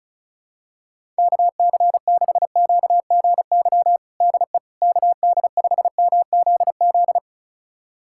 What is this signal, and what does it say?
Morse code, heard as a keyed tone, about 36 wpm, 700 Hz. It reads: KC6QGY DE KD5MZZ
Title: USA Callsign DE Callsign - 6 Characters - 35wpm